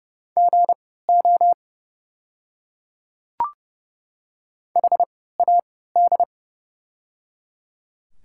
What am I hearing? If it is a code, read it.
Morse code, heard as a keyed tone, about 30 wpm, spaced as for 17 wpm, 700 Hz. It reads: GO HAD
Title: 3x Character Spacing - Top 100 Words - 30wpm